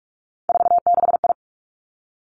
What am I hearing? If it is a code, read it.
Morse code, heard as a keyed tone, about 46 wpm, 700 Hz. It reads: 46I
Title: Three Challenging Letter-Number Combinations 45wpm